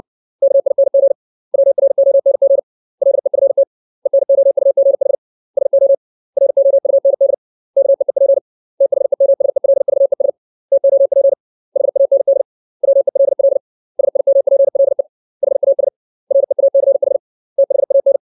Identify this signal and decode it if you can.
Morse code, heard as a keyed tone, about 60 wpm, 550 Hz. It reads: KING WRONG LEFT ENOUGH SO ROUND KEEP THEMSELVES TOP HAND WELL SIMPLE HAS REACH THAN
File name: 632 words / 632 Words 60WPM